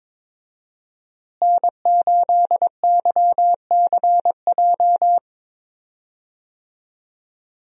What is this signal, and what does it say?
Morse code, heard as a keyed tone, about 22 wpm, 700 Hz. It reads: N8YCJ
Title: USA Call Signs - 5 Characters - Repeated 3x - 22wpm